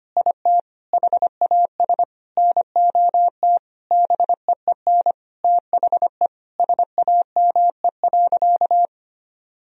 Morse code, heard as a keyed tone, about 25 wpm, 700 Hz. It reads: IT HAS NOT BEEN THE SAME.
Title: Sentences from Top 1000 Words - Review Part 1 - 25wpm